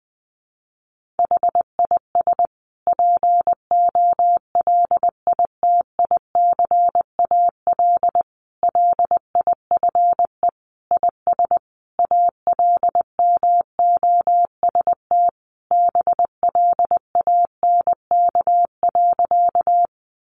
Morse code, heard as a keyed tone, about 20 wpm, 700 Hz. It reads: HIS POLITICAL LIFE IS ALMOST BLANK.
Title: Sentences from Top 1500 Words 20wpm